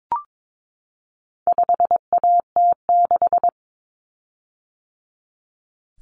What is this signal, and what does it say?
Morse code, heard as a keyed tone, about 22 wpm, 700 Hz. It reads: 5AT6
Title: Common Cut Numbers - 4-Digit 22wpm